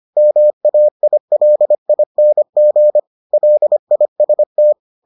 Morse code, heard as a keyed tone, about 25 wpm, 600 Hz. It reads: MAILING LIST